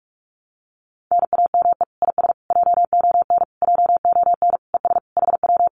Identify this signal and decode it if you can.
Morse code, heard as a keyed tone, about 46 wpm, 700 Hz. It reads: NAME IS JON JON ES HW
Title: Common QSO Phrases Encore 45wpm